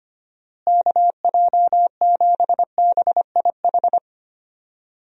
Morse code, heard as a keyed tone, about 25 wpm, 700 Hz. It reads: KJ7BIH
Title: USA Call Signs - 6 Characters - Repeated 3x - 25wpm